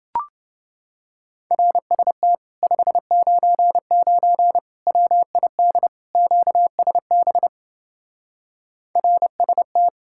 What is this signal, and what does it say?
Morse code, heard as a keyed tone, about 30 wpm, 700 Hz. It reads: RST 599 WID QSB RST